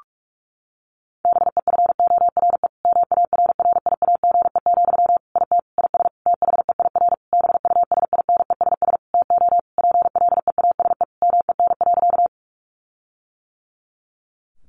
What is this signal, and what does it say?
Morse code, heard as a keyed tone, about 44 wpm, 700 Hz. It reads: BEFORE MARRIAGE, IT IS THEIR BUSINESS TO PLEASE MEN.
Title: Sentences from Top 1500 Words 45wpm